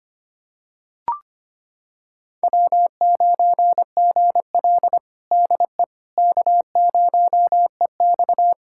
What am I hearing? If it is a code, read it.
Morse code, heard as a keyed tone, about 25 wpm, 700 Hz. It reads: W9GL DE K0EX
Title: Practice QSO - Exchange 4 - Ending - 25wpm